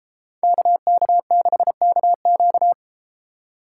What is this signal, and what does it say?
Morse code, heard as a keyed tone, about 32 wpm, 700 Hz. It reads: KK6KQ